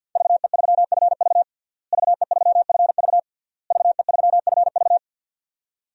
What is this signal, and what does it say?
Morse code, heard as a keyed tone, about 46 wpm, 700 Hz. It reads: VE3FV VE3FV VE3FV
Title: DXCC-100 Call Signs - 5 Characters - Repeated 3x - 50wpm